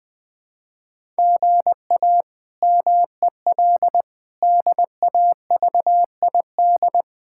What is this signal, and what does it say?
Morse code, heard as a keyed tone, about 21 wpm, 700 Hz. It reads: GA MEL DAVID
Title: K1USN SST QSO Practice - 20wpm